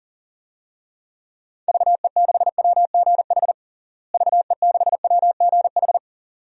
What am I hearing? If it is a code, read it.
Morse code, heard as a keyed tone, about 40 wpm, 700 Hz. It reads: VE6WGH VE6WGH
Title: DXCC-100 Call Signs - 6 Characters - Repeated 2x - 40wpm